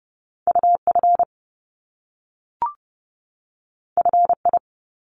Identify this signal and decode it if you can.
Morse code, heard as a keyed tone, about 30 wpm, 700 Hz. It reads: UF FI
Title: Top Two Letter Combinations in Words 30wpm